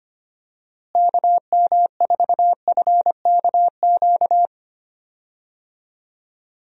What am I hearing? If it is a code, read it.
Morse code, heard as a keyed tone, about 25 wpm, 700 Hz. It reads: KM4FKQ
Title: USA Call Signs - 6 Characters - Repeated 2x - 25wpm